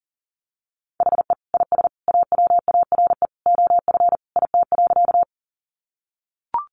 Morse code, heard as a keyed tone, about 40 wpm, 700 Hz. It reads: HE IS AWARE OF IT.
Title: Sentences from Top 1200 Words 40wpm